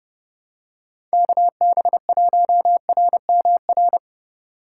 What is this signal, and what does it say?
Morse code, heard as a keyed tone, about 29 wpm, 700 Hz. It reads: KB1RMR